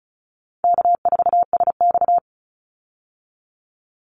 Morse code, heard as a keyed tone, about 34 wpm, 700 Hz. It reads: K4SX
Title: Easy Call Signs Repeated 2x - 35wpm